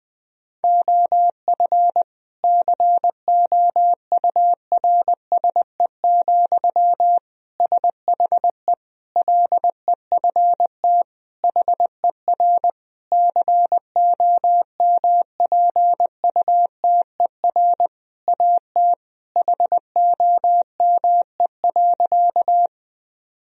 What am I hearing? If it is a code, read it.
Morse code, heard as a keyed tone, about 20 wpm, 700 Hz. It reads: OF COURSE, SHE LEFT HER COMPUTER AT HOME.
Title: Sentences from Top 1000 Words - Review Part 3 - 20wpm